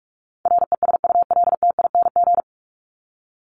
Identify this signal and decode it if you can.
Morse code, heard as a keyed tone, about 44 wpm, 700 Hz. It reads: RESULTING
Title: ING Suffix - Top 100 Words with Base Words 45wpm